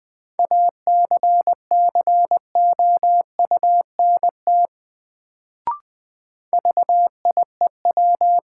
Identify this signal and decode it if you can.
Morse code, heard as a keyed tone, about 20 wpm, 700 Hz. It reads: ACCOUNT VIEW
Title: Sentences from Top 900 Words - New Word Drill - 20wpm